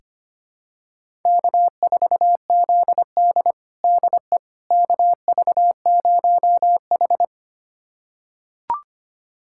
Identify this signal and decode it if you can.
Morse code, heard as a keyed tone, about 25 wpm, 700 Hz. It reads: K4ZD DE KV0H
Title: USA Callsign DE Callsign - 4 Characters - 25wpm